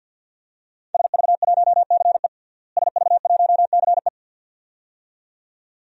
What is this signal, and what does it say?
Morse code, heard as a keyed tone, about 49 wpm, 700 Hz. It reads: SV1CE SV1CE
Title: DXCC-100 Call Signs - 5 Characters - Repeated 2x - 50wpm